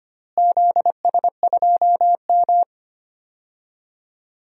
Morse code, heard as a keyed tone, about 25 wpm, 700 Hz. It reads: ZS2M
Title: DXCC-100 Call Signs - 4 Characters - Repeated 3x - 25wpm